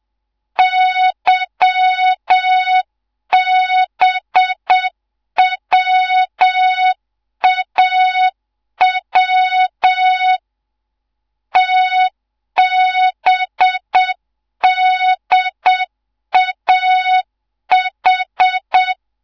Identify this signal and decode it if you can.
Morse code, heard as a keyed tone, about 7 wpm, 3700 Hz. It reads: YBWAW TBDAH